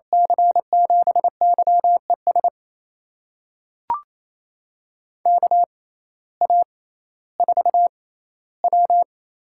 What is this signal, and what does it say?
Morse code, heard as a keyed tone, about 28 wpm, spaced as for 15 wpm, 700 Hz. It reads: C7YES KA4W
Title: USA Call Signs - 6 Characters - 6x Character Spacing - 28wpm